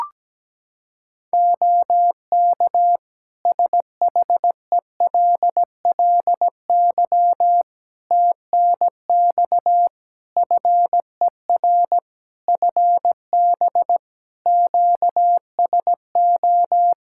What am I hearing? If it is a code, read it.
Morse code, heard as a keyed tone, about 17 wpm, 700 Hz. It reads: OK SHELLY TNX FER FB QSO
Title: Practice QSO - Exchange 4 - Ending - 17wpm